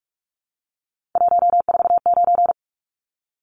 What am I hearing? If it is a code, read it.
Morse code, heard as a keyed tone, about 44 wpm, 700 Hz. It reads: 149
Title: Numbers - 3-Digit 45wpm